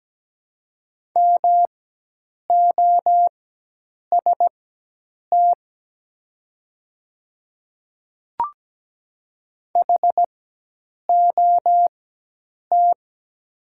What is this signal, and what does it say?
Morse code, heard as a keyed tone, about 17 wpm, spaced as for 8 wpm, 700 Hz. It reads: MOST HOT
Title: 4x Character Spacing - Top 100 Words - 17wpm